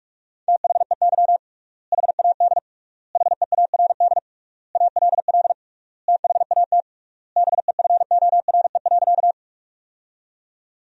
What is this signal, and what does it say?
Morse code, heard as a keyed tone, about 46 wpm, 700 Hz. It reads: THEY HAD HEARD ALL THAT BEFORE.
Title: 3x Word-Spacing - Sentences from Top 400 Words - 45wpm